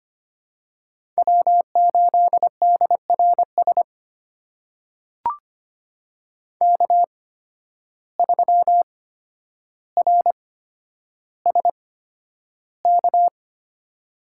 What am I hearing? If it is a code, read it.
Morse code, heard as a keyed tone, about 25 wpm, spaced as for 11 wpm, 700 Hz. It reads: W8DRS K3RSK
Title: USA Call Signs - 5 Characters - 8x Character Spacing - 25wpm